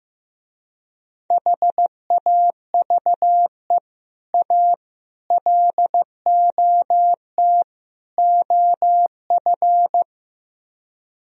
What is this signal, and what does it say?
Morse code, heard as a keyed tone, about 16 wpm, 700 Hz. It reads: HAVE A LOT OF
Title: Speed-Racing - Sets of 4 Words 15wpm